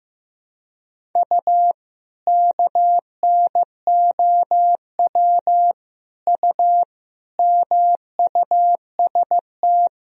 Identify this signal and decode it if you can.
Morse code, heard as a keyed tone, about 15 wpm, 700 Hz. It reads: U KNOW U MUST
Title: Sentences from Top 500 Words and Abbreviations 15wpm